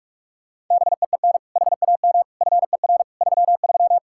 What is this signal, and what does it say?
Morse code, heard as a keyed tone, about 46 wpm, 700 Hz. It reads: BEEN HAM FER 33
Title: Practice QSO - Exchange 3 - More Info - 45wpm